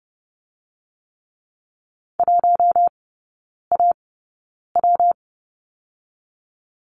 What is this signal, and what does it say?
Morse code, heard as a keyed tone, about 30 wpm, 700 Hz. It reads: 1 A W
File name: Lesson 19e - Letter w - Three Characters 30wpm